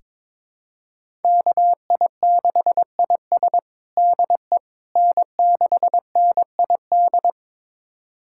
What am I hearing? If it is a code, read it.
Morse code, heard as a keyed tone, about 22 wpm, 700 Hz. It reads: KI6IS DE N6NID
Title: USA Callsign DE Callsign - 5 Characters - 22wpm